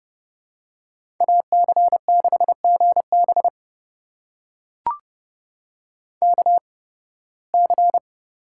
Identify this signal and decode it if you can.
Morse code, heard as a keyed tone, about 30 wpm, spaced as for 20 wpm, 700 Hz. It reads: AC6GB KC
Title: USA Call Signs - 8x Character Spacing - 30wpm